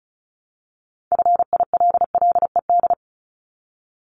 Morse code, heard as a keyed tone, about 36 wpm, 700 Hz. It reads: FILLED
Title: Multi-letter Phonemes Repeated 3x 35wpm